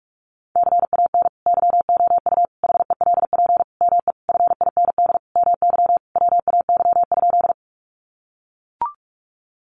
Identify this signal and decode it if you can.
Morse code, heard as a keyed tone, about 44 wpm, 700 Hz. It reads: CAN YOU HELP ME FIND MY WAY?
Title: Sentences from Top 200 Words 45wpm